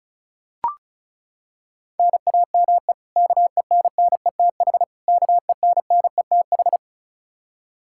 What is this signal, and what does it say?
Morse code, heard as a keyed tone, about 34 wpm, 700 Hz. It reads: NAME KENNETH KENNETH
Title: Practice QSO - Exchange 1 - Beginning - 35wpm